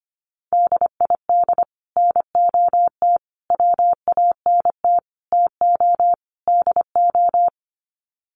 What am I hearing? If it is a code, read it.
Morse code, heard as a keyed tone, about 25 wpm, 700 Hz. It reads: DID NOT WANT TO DO
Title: Sets of 5 Words Encore 25wpm